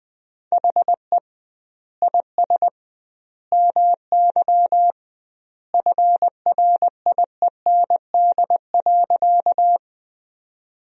Morse code, heard as a keyed tone, about 20 wpm, 700 Hz. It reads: HE IS MY FRIEND.
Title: 2x Word-Spacing - Sentences from Top 300 Words - 20wpm